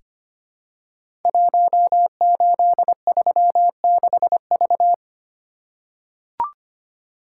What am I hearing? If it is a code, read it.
Morse code, heard as a keyed tone, about 25 wpm, 700 Hz. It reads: 1836V
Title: Code Groups - 5 Letter:Numbers - 25wpm